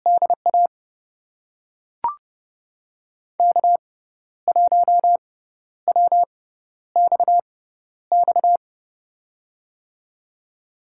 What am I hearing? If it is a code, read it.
Morse code, heard as a keyed tone, about 30 wpm, spaced as for 12 wpm, 700 Hz. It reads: DA K1WXX
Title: USA Call Signs - 5 Characters - 6x Character Spacing - 30wpm